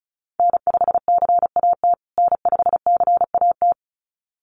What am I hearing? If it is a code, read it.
Morse code, heard as a keyed tone, about 34 wpm, 700 Hz. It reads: N5CAT N5CAT
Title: USA Call Signs Repeated 2x 35wpm